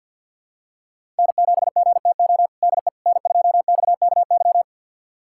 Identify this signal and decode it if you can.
Morse code, heard as a keyed tone, about 49 wpm, 700 Hz. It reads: N7GTO DE N2XKY
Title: USA Callsign DE Callsign - 5 Characters - 50wpm